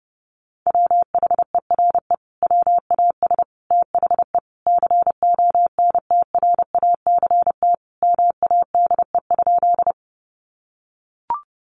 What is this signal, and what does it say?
Morse code, heard as a keyed tone, about 30 wpm, 700 Hz. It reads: WHERE WAS THE CONTRACT MADE?
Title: Sentences from Top 1400 Words 30wpm